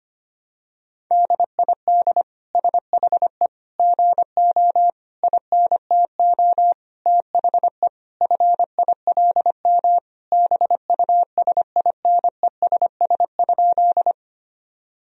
Morse code, heard as a keyed tone, about 25 wpm, 700 Hz. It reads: DID SHE GO INTO THE FILM BUSINESS?
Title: Sentences from Top 1000 Words - Review Part 3 - 25wpm